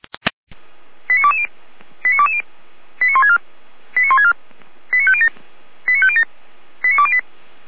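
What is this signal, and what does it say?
Five-tone selective call (ZVEI), heard as quick runs of five tones, tones of about 70 ms. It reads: 89220 89220 89164 89164 89507 89507 89208
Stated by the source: ZVEI five-tone call received by SDR.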